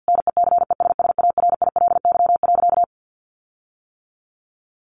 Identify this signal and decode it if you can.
Morse code, heard as a keyed tone, about 50 wpm, 700 Hz. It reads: NECESSARILY.
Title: Sentences from Top 1400 Words 50wpm